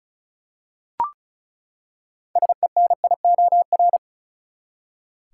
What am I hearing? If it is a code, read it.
Morse code, heard as a keyed tone, about 36 wpm, 700 Hz. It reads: SENIOR
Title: Six Letter Words - 35wpm